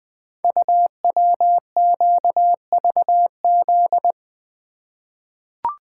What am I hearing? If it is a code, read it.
Morse code, heard as a keyed tone, about 20 wpm, 700 Hz. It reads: UWQVZ
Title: Code Groups - 5 Letters - 20wpm